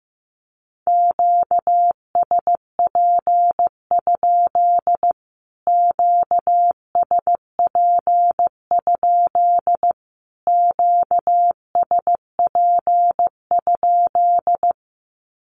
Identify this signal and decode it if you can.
Morse code, heard as a keyed tone, about 15 wpm, 700 Hz. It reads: QSP? QSP? QSP?
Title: Q-Codes Repeated 3x 15wpm